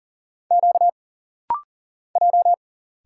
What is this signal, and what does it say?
Morse code, heard as a keyed tone, about 40 wpm, 700 Hz. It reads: Q J
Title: Single Letter-Number - Rapid-Fire 40wpm